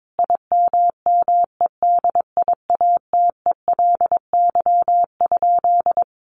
Morse code, heard as a keyed tone, about 22 wpm, 700 Hz. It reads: IMMEDIATELY?
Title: Sentences from Top 1300 Words 22wpm